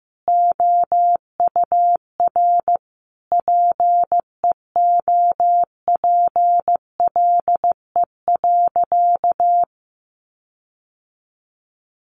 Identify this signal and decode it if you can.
Morse code, heard as a keyed tone, about 15 wpm, 700 Hz. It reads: OUR PEOPLE.
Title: Sentences from Top 1500 Words 15wpm